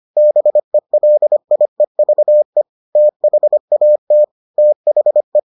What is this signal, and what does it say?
Morse code, heard as a keyed tone, about 25 wpm, 600 Hz. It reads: BELIEVE THAT THE